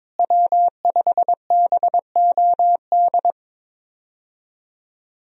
Morse code, heard as a keyed tone, about 22 wpm, 690 Hz. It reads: W5BOD
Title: USA Call Signs - 5 Characters - Repeated 3x - 22wpm